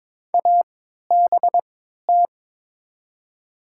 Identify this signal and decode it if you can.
Morse code, heard as a keyed tone, about 22 wpm, spaced as for 13 wpm, 700 Hz. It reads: ABT
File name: Top 90 QSO Elements - 3x Character Spacing - 22wpm